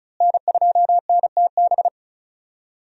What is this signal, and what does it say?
Morse code, heard as a keyed tone, about 36 wpm, 700 Hz. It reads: N2NTB
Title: USA Call Signs - 5 Characters - Repeated 3x - 35wpm